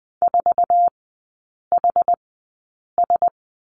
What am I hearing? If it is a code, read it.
Morse code, heard as a keyed tone, about 20 wpm, spaced as for 8 wpm, 690 Hz. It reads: 4HS
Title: 081 Lesson 14e - Number 2 - Three Characters 20wpm